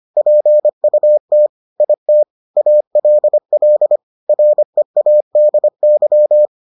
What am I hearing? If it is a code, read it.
Morse code, heard as a keyed tone, about 25 wpm, 600 Hz. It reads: PUT IT ALL READY